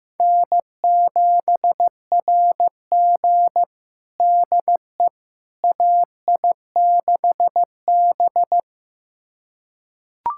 Morse code, heard as a keyed tone, about 15 wpm, 700 Hz. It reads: N7RG DE AI6B
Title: USA Callsign DE Callsign - 4 Characters - 15wpm